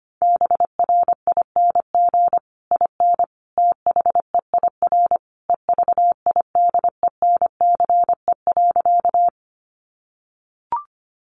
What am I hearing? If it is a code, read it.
Morse code, heard as a keyed tone, about 25 wpm, 700 Hz. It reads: BRING IN THEIR EVIDENCE.
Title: Sentences from Top 1300 Words 25wpm